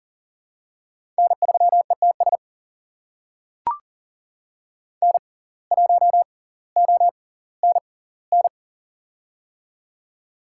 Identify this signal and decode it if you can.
Morse code, heard as a keyed tone, about 40 wpm, spaced as for 20 wpm, 700 Hz. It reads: N3ETS N1ONN
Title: USA Call Signs - 5 Characters - 6x Character Spacing - 40wpm